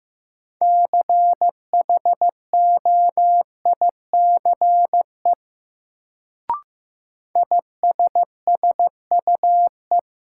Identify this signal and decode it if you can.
Morse code, heard as a keyed tone, about 15 wpm, 700 Hz. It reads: CHOICE ISSUE
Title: Sentences from Top 900 Words - New Word Drill - 15wpm